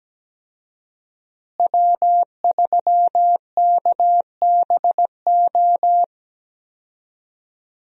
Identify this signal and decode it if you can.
Morse code, heard as a keyed tone, about 17 wpm, 700 Hz. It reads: W3KBO
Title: USA Call Signs - 5 Characters - Repeated 3x - 17wpm